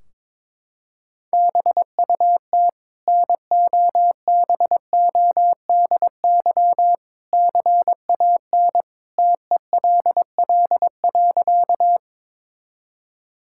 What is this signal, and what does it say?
Morse code, heard as a keyed tone, about 22 wpm, 700 Hz. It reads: BUT NOBODY CAN TELL.
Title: Sentences from Top 1600 Words 22wpm